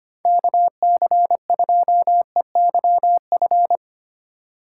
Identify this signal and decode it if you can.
Morse code, heard as a keyed tone, about 25 wpm, 700 Hz. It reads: KC2EYF
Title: USA Call Signs - 6 Characters - 4x Character Spacing - 25wpm